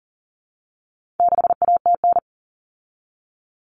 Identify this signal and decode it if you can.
Morse code, heard as a keyed tone, about 40 wpm, 700 Hz. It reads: 6ATN
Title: All Cut Numbers - 4-Digit 40wpm